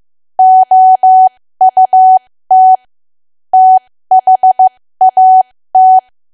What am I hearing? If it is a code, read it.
Morse code, heard as a keyed tone, about 15 wpm, spaced as for 13 wpm, 750 Hz. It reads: OUT THAT
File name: cwa / W1AW-Level2 / 13WPM